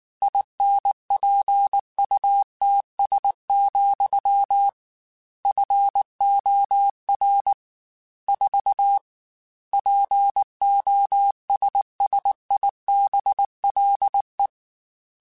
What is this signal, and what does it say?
Morse code, heard as a keyed tone, about 19 wpm, 800 Hz. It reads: INPUTS, FOR 4 POSSIBLE
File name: Technical Subjects / Technical Subjects-mp3s / AND Gate @18